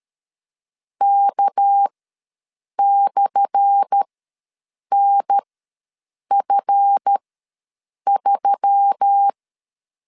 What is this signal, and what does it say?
Morse code, heard as a keyed tone, about 13 wpm, spaced as for 7 wpm, 800 Hz. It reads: K/NF3
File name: LEZIONI / 10 SIGNS / 10A SLASH ws 12wpm 3443